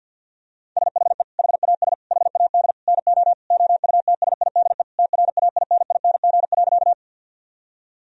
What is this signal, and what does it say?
Morse code, heard as a keyed tone, about 50 wpm, 700 Hz. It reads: SHE HAS HAD NO OUTSIDE TRAINING.
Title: Sentences from Top 1000 Words - Review Part 4 - 50wpm